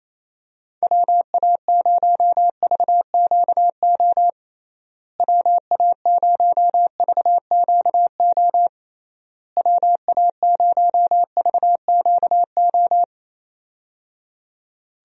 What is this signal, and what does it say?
Morse code, heard as a keyed tone, about 28 wpm, 700 Hz. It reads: WA0VQO WA0VQO WA0VQO
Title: USA Call Signs - 6 Characters - Repeated 3x - 28wpm